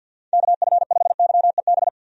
Morse code, heard as a keyed tone, about 46 wpm, 700 Hz. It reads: KF5YEB
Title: USA Call Signs - 6 Characters - 50wpm